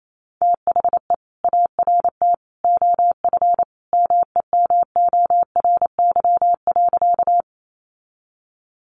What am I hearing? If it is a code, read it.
Morse code, heard as a keyed tone, about 28 wpm, 700 Hz. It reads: THE ART OF MEMORY.
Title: Sentences from Top 1300 Words 28wpm